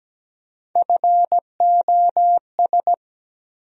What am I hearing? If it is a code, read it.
Morse code, heard as a keyed tone, about 17 wpm, 700 Hz. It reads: FOS
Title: USA Call Sign Postfixes - 3 Letters 17wpm